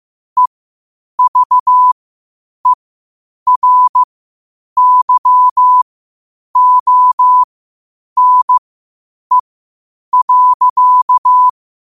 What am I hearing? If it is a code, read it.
Morse code, heard as a keyed tone, about 15 wpm, spaced as for 8 wpm, 1000 Hz. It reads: EVERYONE.
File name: qotdfiles / 8wpm